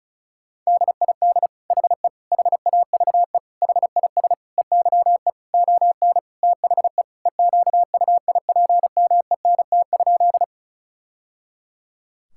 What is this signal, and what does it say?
Morse code, heard as a keyed tone, about 36 wpm, 700 Hz. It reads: DID HE HAVE HIS EYE ON THE EQUIPMENT?
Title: Sentences from Top 1000 Words - Review Part 5 - 35wpm